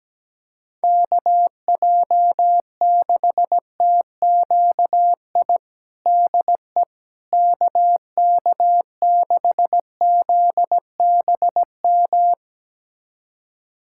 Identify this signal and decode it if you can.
Morse code, heard as a keyed tone, about 17 wpm, 700 Hz. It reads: KJ6TQI DE KK6ZBM
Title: USA Callsign DE Callsign 17wpm